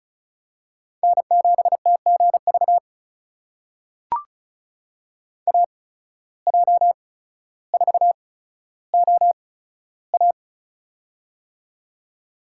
Morse code, heard as a keyed tone, about 36 wpm, spaced as for 15 wpm, 700 Hz. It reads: N7TGV AJ4OA
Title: USA Call Signs - 5 Characters - 8x Character Spacing - 35wpm